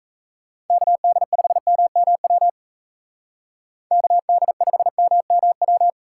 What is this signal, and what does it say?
Morse code, heard as a keyed tone, about 36 wpm, 700 Hz. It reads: KD5MMW KD5MMW